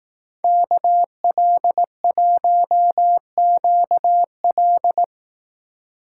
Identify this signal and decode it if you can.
Morse code, heard as a keyed tone, about 18 wpm, 700 Hz. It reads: KL1QL
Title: Speed Racing - USA 2x2 Call Signs 15wpm